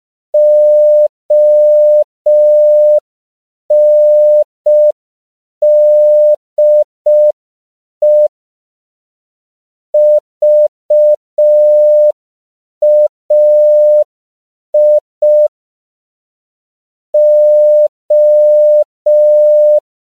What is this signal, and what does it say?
Morse code, heard as a keyed tone, about 5 wpm, 600 Hz. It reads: ONDE VAI O
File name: CW / 5PPM